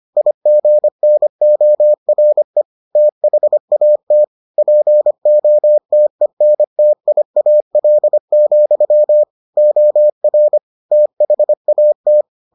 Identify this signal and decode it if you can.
Morse code, heard as a keyed tone, about 25 wpm, 600 Hz. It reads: IGNORE THAT POTENTIAL, OR THAT